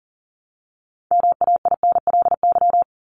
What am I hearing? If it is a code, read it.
Morse code, heard as a keyed tone, about 40 wpm, 700 Hz. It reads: MAINLY
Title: Top 1251-1500 Words 40wpm